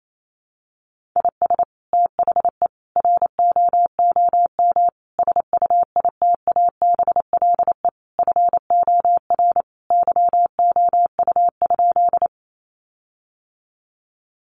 Morse code, heard as a keyed tone, about 28 wpm, 700 Hz. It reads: IS THE ROOM SUITABLE FOR YOU?
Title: Sentences from Top 1400 Words 28wpm